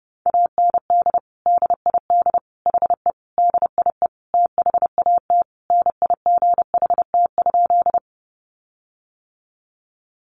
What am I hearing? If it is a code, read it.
Morse code, heard as a keyed tone, about 30 wpm, 700 Hz. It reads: AND DID HE DIE THAT NIGHT?
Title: Sentences from Top 1300 Words 30wpm